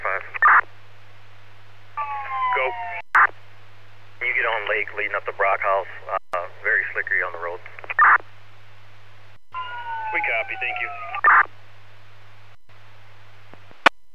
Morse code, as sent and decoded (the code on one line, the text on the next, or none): none